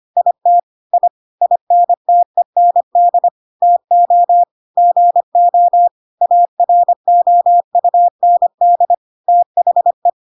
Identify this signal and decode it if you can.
Morse code, heard as a keyed tone, about 25 wpm, 700 Hz. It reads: IT I INTEND TO GO AROUND THE
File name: Pinocchio / Pinocchio ch2 25 25 ik1hge